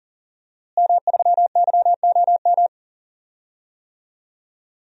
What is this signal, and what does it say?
Morse code, heard as a keyed tone, about 38 wpm, 700 Hz. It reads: M3YOM
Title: DXCC-100 Call Signs - 5 Characters - Repeated 3x - 40wpm